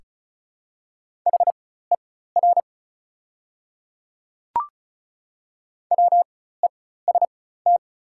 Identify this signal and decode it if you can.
Morse code, heard as a keyed tone, about 36 wpm, spaced as for 16 wpm, 690 Hz. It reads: HER WEST
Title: 4x Character Spacing - Top 500 Words - 35wpm